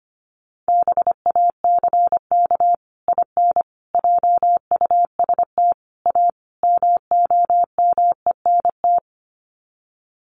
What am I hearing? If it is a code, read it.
Morse code, heard as a keyed tone, about 25 wpm, 690 Hz. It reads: BACK IN JUST A MOMENT
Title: Sets of 5 Words Encore 25wpm